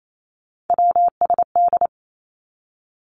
Morse code, heard as a keyed tone, about 28 wpm, 700 Hz. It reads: WSD
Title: Three Challenging Letter-Number Combinations 28wpm